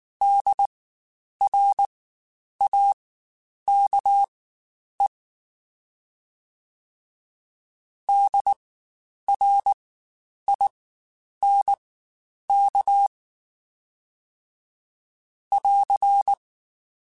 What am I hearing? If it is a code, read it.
Morse code, heard as a keyed tone, about 19 wpm, spaced as for 9 wpm, 800 Hz. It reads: DRAKE DRINK <AR>
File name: Spaced Alphas / Spaced Alphas mp3s / Ds sp @18